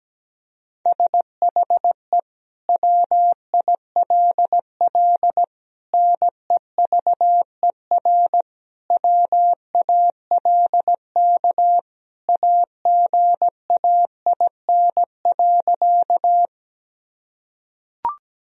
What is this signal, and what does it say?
Morse code, heard as a keyed tone, about 17 wpm, 700 Hz. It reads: SHE WILL NEVER WALK AGAIN.
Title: Sentences from Top 1000 Words - Review Part 1 - 17wpm